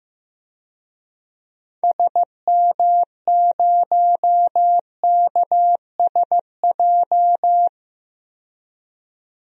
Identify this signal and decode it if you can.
Morse code, heard as a keyed tone, about 16 wpm, 700 Hz. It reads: SM0KSJ
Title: DXCC-100 Call Signs - 6 Characters - 15wpm